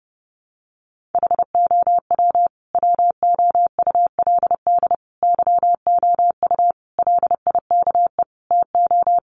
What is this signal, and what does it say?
Morse code, heard as a keyed tone, about 30 wpm, 700 Hz. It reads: HOW WOULD YOU LIKE TO
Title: Sentences from Top 100 Words 30wpm